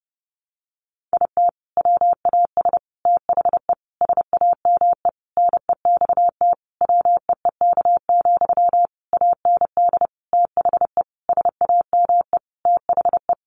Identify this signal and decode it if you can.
Morse code, heard as a keyed tone, about 30 wpm, 700 Hz. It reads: IT WAS THE SAME NEXT WEEK, AND THE SAME THE
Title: Sentences from Top 500 Words 30wpm